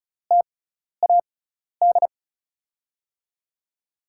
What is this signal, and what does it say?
Morse code, heard as a keyed tone, about 36 wpm, spaced as for 12 wpm, 700 Hz. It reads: TAD
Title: USA Call Signs - 6 Characters - 6x Character Spacing - 35wpm